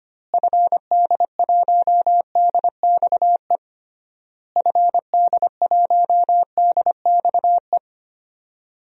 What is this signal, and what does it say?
Morse code, heard as a keyed tone, about 25 wpm, 700 Hz. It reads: FD1DXE FD1DXE
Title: DXCC-100 Call Signs - 6 Characters - Repeated 3x - 25wpm